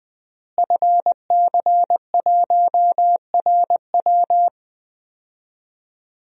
Morse code, heard as a keyed tone, about 20 wpm, 700 Hz. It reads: FC1RW
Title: DXCC-100 Call Signs - 5 Characters - Repeated 3x - 20wpm